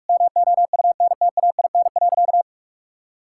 Morse code, heard as a keyed tone, about 41 wpm, 700 Hz. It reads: MOUNTAIN.
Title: Sentences from Top 1000 Words - Review Part 5 - 45wpm